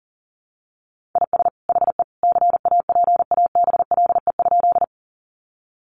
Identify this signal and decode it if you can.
Morse code, heard as a keyed tone, about 40 wpm, 700 Hz. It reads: IS HE CAPABLE?
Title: Sentences from Top 1200 Words 40wpm